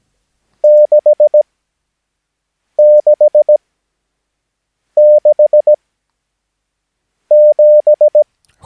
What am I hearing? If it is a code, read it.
Morse code, heard as a keyed tone, about 17 wpm, spaced as for 5 wpm, 600 Hz. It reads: 666 7
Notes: Övningsmaterial med alfabetet A till Ö samt siffror 0 till 9 enligt International/Continental Morse Code.